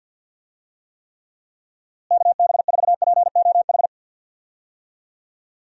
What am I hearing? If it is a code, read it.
Morse code, heard as a keyed tone, about 49 wpm, 700 Hz. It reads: KB4POH
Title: USA Call Signs - 6 Characters - Repeated 3x - 50wpm